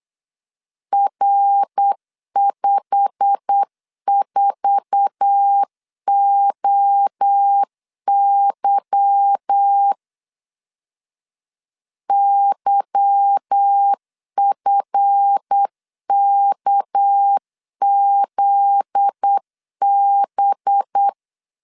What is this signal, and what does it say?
Morse code, heard as a keyed tone, about 8 wpm, 800 Hz. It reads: R54OY YFKZB
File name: LEZIONI / 10 SIGNS / 10B INT 8wpm 3534